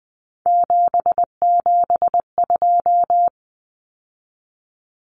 Morse code, heard as a keyed tone, about 20 wpm, 700 Hz. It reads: 772
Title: Numbers - 3-Digit 20wpm